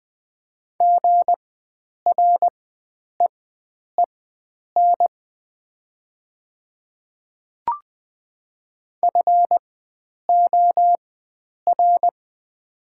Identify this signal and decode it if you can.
Morse code, heard as a keyed tone, about 20 wpm, spaced as for 9 wpm, 700 Hz. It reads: GREEN FOR
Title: 4x Character Spacing - Top 500 Words - 20wpm